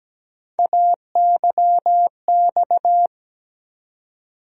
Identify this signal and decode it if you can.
Morse code, heard as a keyed tone, about 17 wpm, 700 Hz. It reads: AYX
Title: USA Call Sign Postfixes - 3 Letters 17wpm